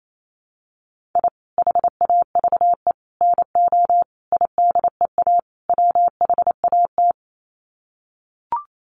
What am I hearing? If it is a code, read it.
Morse code, heard as a keyed tone, about 28 wpm, 700 Hz. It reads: I HAVE NO IDEA WHAT
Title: Sets of 5 Words Encore 28wpm